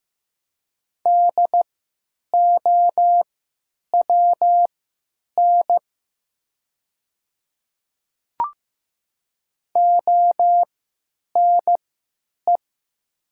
Morse code, heard as a keyed tone, about 15 wpm, spaced as for 9 wpm, 690 Hz. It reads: DOWN ONE
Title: 3x Character Spacing - Top 100 Words - 15wpm